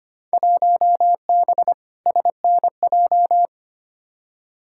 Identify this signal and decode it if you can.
Morse code, heard as a keyed tone, about 25 wpm, 700 Hz. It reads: 1B SNJ
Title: Field Day Exchanges - 25wpm